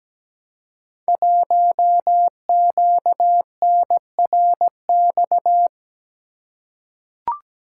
Morse code, heard as a keyed tone, about 17 wpm, 700 Hz. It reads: 1QNRX
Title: Code Groups - 5 Letter:Numbers - 17wpm